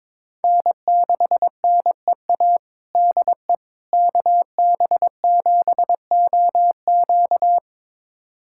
Morse code, heard as a keyed tone, about 22 wpm, 700 Hz. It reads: N6NEA DE KB7OQ
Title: USA Callsign DE Callsign - 5 Characters - 22wpm